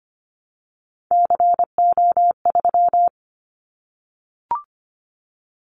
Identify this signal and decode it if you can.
Morse code, heard as a keyed tone, about 25 wpm, 700 Hz. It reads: CO3
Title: Callsign Prefixes for Top 100 DXCC - 25wpm